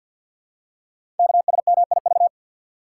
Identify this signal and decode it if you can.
Morse code, heard as a keyed tone, about 50 wpm, 700 Hz. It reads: KSMIV
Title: Code Groups - 5 Letters - 50wpm